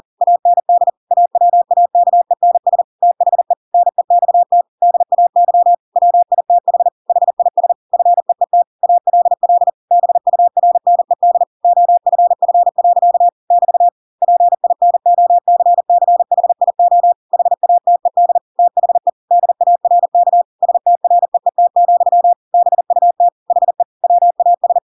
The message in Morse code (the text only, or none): AND AWAKENS THE NEXT DAY WITH HIS FEET ALL BURNED OFF. <BT> PINOCCHIO HATED THE DARK STREET, BUT HE WAS